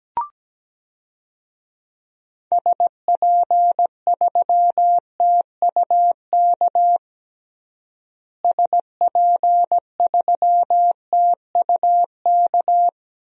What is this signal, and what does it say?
Morse code, heard as a keyed tone, about 17 wpm, 700 Hz. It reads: SP3TUK SP3TUK
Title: DXCC-100 Call Signs - 6 Characters - Repeated 3x - 17wpm